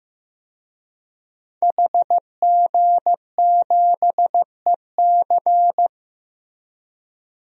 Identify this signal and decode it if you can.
Morse code, heard as a keyed tone, about 15 wpm, 700 Hz. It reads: HG7EC
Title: DXCC-100 Call Signs - 5 Characters - Repeated 3x - 15wpm